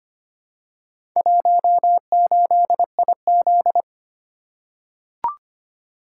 Morse code, heard as a keyed tone, about 25 wpm, 700 Hz. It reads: 18IZ
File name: practice / 25wpm / Four Letter-Number Combinations - 25wpm